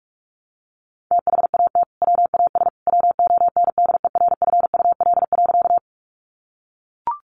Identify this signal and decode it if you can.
Morse code, heard as a keyed tone, about 44 wpm, 700 Hz. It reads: THAT WAS WONDERFUL.
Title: Sentences from Top 1100 Words 45wpm